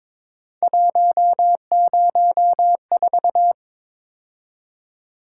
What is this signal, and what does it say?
Morse code, heard as a keyed tone, about 26 wpm, 700 Hz. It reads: 104
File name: Speed-Racing - Three Letter-Number Combinations - 22wpm